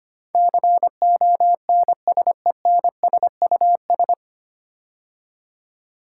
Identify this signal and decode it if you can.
Morse code, heard as a keyed tone, about 25 wpm, 690 Hz. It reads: CONSENSUS
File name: practice / 25wpm / 9 Letter Words - First 3 Letters Given - 25wpm